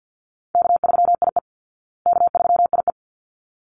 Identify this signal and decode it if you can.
Morse code, heard as a keyed tone, about 48 wpm, 700 Hz. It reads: K3IE K3IE
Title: Call Signs Repeated 3x - 50wpm